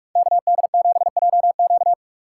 Morse code, heard as a keyed tone, about 44 wpm, 700 Hz. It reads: KD7JQ
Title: USA Call Signs - 5 Characters - 4x Character Spacing - 45wpm